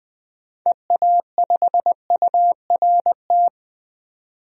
Morse code, heard as a keyed tone, about 20 wpm, 700 Hz. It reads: EA5URT
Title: DXCC-100 Call Signs - 6 Characters - 20wpm